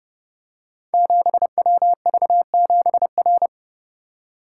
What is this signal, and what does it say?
Morse code, heard as a keyed tone, about 28 wpm, 700 Hz. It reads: 7WV7R
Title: Code Groups - 5 Letter:Numbers - 30wpm